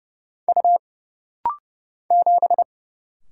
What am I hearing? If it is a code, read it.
Morse code, heard as a keyed tone, about 30 wpm, 700 Hz. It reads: U 7
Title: Single Letter-Number Alt - Rapid-Fire 30wpm